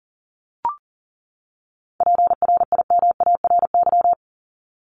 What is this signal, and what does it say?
Morse code, heard as a keyed tone, about 40 wpm, 700 Hz. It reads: PRIMARY
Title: LY Suffix - Top 100 Words with Base Words 40wpm